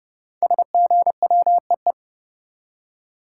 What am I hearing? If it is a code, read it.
Morse code, heard as a keyed tone, about 29 wpm, 700 Hz. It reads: SGWEE
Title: Code Groups - 5 Letters - 30wpm